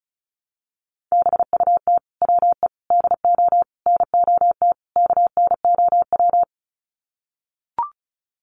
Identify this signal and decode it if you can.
Morse code, heard as a keyed tone, about 34 wpm, 700 Hz. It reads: BUT WE DO NOT KNOW
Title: Sets of 5 Words Encore2 35wpm